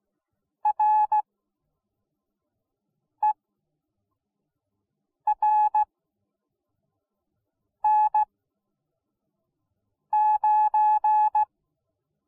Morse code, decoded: RERN9

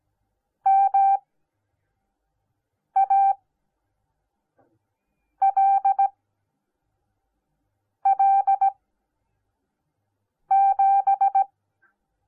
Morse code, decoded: MALL7

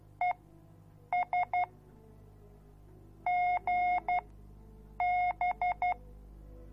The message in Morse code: ES GB

